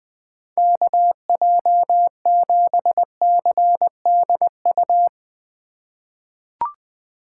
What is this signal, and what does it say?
Morse code, heard as a keyed tone, about 21 wpm, 700 Hz. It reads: KJ7CDU
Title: USA Call Signs - 4x Character Spacing - 20wpm